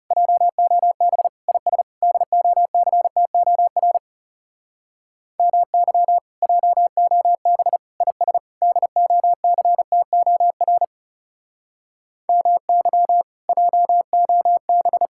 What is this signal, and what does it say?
Morse code, heard as a keyed tone, about 40 wpm, 700 Hz. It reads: JOB IS DOCTOR MY JOB IS DOCTOR MY JOB